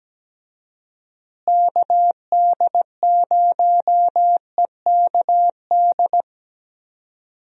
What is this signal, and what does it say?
Morse code, heard as a keyed tone, about 17 wpm, 700 Hz. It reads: KD0EKD
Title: USA Call Signs - 6 Characters - Repeated 3x - 17wpm